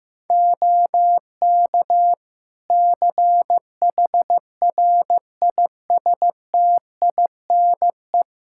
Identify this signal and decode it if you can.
Morse code, heard as a keyed tone, about 15 wpm, 700 Hz. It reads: OK CHRISTINE
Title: Practice QSO - Exchange 4 - Ending - 15wpm